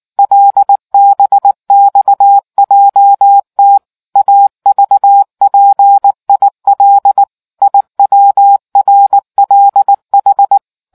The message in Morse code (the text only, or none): LBXJT AVPIL IWRLH